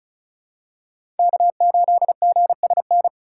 Morse code, heard as a keyed tone, about 34 wpm, 700 Hz. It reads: K8GSN
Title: USA Call Signs - 5 Characters - 4x Character Spacing - 35wpm